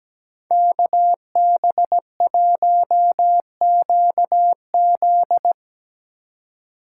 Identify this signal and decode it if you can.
Morse code, heard as a keyed tone, about 19 wpm, 700 Hz. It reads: KB1QZ
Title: Speed Racing - USA Call Signs 17wpm